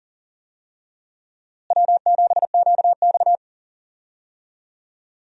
Morse code, heard as a keyed tone, about 40 wpm, 700 Hz. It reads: W7QX
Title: USA Call Signs - 4 Characters - Repeated 2x - 40wpm